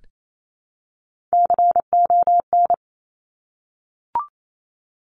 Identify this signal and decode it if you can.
Morse code, heard as a keyed tone, about 28 wpm, 700 Hz. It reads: CON
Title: Top Three Letter Combinations in Words 28wpm